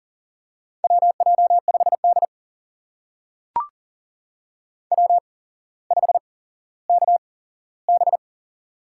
Morse code, heard as a keyed tone, about 40 wpm, spaced as for 17 wpm, 700 Hz. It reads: WJ5D W5KB
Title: USA Call Signs - 8x Character Spacing - 40wpm